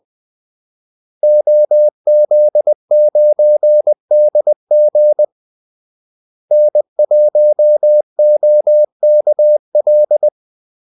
Morse code, heard as a keyed tone, about 20 wpm, 600 Hz. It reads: OZ9DG N1OKL
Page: Kaldesignaler 20 wpm